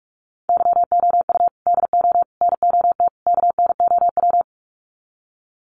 Morse code, heard as a keyed tone, about 44 wpm, 700 Hz. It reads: YOU DO NOT KNOW
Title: Sets of 4 Words 45wpm